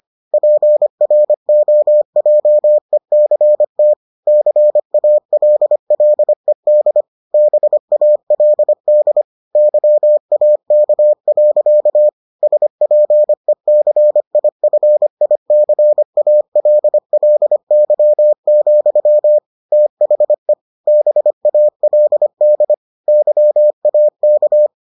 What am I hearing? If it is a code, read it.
Morse code, heard as a keyed tone, about 25 wpm, 600 Hz. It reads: PROJECT CALLED BALD YAK. SPECIFICALLY, THE BALD YAK